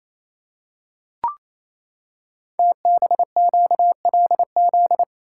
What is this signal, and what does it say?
Morse code, heard as a keyed tone, about 28 wpm, 690 Hz. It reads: TBQLZ
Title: Code Groups - 5 Letter:Numbers - 28wpm